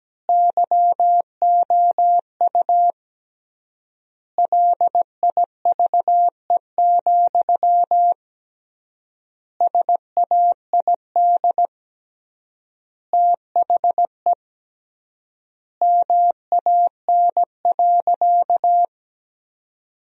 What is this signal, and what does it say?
Morse code, heard as a keyed tone, about 17 wpm, 700 Hz. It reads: YOU LIVE, SAID THE MAN.
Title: 3x Word-Spacing - Sentences from Top 200 Words - 17wpm